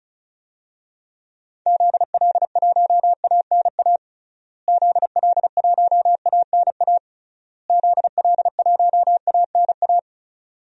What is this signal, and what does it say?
Morse code, heard as a keyed tone, about 34 wpm, 700 Hz. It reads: ZL1ANA ZL1ANA ZL1ANA
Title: DXCC-100 Call Signs - 6 Characters - Repeated 3x - 35wpm